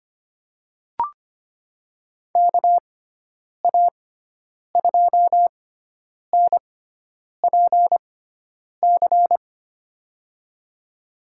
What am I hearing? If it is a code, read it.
Morse code, heard as a keyed tone, about 25 wpm, spaced as for 9 wpm, 700 Hz. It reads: KA2NPC
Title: USA Call Signs - 6 Characters - 6x Character Spacing - 25wpm